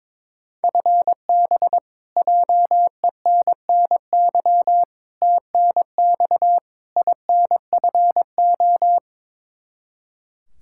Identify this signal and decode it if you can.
Morse code, heard as a keyed tone, about 22 wpm, 700 Hz. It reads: FB JENNY TNX INFO
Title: Practice QSO - Exchange 3 - More Info - 22wpm